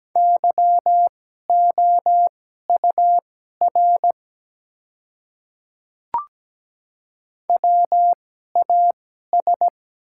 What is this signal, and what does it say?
Morse code, heard as a keyed tone, about 17 wpm, spaced as for 12 wpm, 700 Hz. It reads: YOUR WAS
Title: 2x Character Spacing - Top 100 Words - 17wpm